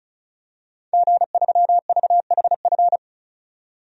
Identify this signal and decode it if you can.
Morse code, heard as a keyed tone, about 34 wpm, 700 Hz. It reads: G3VHF
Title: DXCC-100 Call Signs - 5 Characters - 35wpm